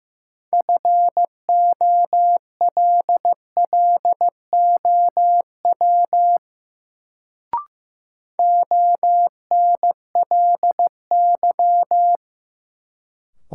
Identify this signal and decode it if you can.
Morse code, heard as a keyed tone, about 15 wpm, 700 Hz. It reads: FOLLOW ONLY
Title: Sentences from Top 200 Words - New Word Drill - 15wpm